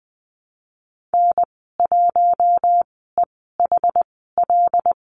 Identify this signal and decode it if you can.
Morse code, heard as a keyed tone, about 20 wpm, spaced as for 15 wpm, 700 Hz. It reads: N1EHL
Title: 088 Lesson 15g - Number 5 - Call Signs 20wpm